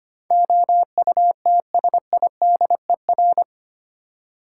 Morse code, heard as a keyed tone, about 25 wpm, 700 Hz. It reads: OUTSIDER
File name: practice / 25wpm / 8 Letter Words - First 3 Letters Given - 25wpm